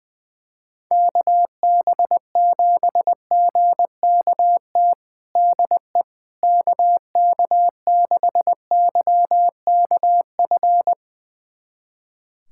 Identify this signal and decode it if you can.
Morse code, heard as a keyed tone, about 20 wpm, 700 Hz. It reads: KB7GKT DE KK6YKF
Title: USA Callsign DE Callsign 20wpm